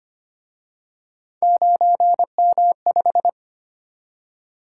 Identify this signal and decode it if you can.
Morse code, heard as a keyed tone, about 31 wpm, 700 Hz. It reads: 9M5
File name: Speed-Racing - Three Letter-Number Combinations - 25wpm